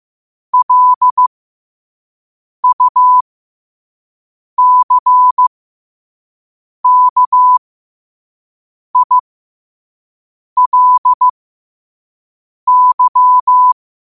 Morse code, heard as a keyed tone, about 15 wpm, spaced as for 5 wpm, 1000 Hz. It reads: LUCKILY
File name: qotdfiles / 5wpm